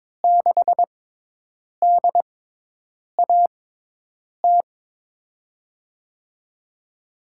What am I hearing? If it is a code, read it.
Morse code, heard as a keyed tone, about 22 wpm, spaced as for 8 wpm, 700 Hz. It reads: 6DAT
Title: USA Call Signs - 6 Characters - 6x Character Spacing - 22wpm